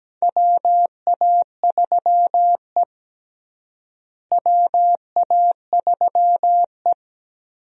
Morse code, heard as a keyed tone, about 17 wpm, 700 Hz. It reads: WA3E WA3E
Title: USA Call Signs - 4 Characters - Repeated 2x - 17wpm